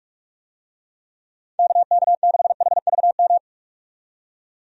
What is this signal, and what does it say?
Morse code, heard as a keyed tone, about 46 wpm, 700 Hz. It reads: KK6HVM
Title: USA Call Signs - 6 Characters - 45wpm